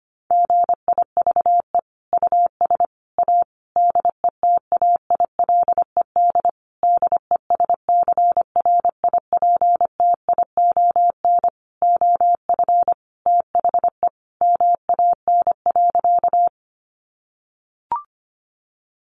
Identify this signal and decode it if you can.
Morse code, heard as a keyed tone, about 25 wpm, 700 Hz. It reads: GIVE US A DETAILED DESCRIPTION OF THE MAN.
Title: Sentences from Top 1500 Words 25wpm